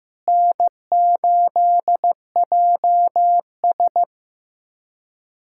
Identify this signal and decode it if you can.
Morse code, heard as a keyed tone, about 15 wpm, 700 Hz. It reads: N8JS
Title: K1USN SST QSO Practice - 15wpm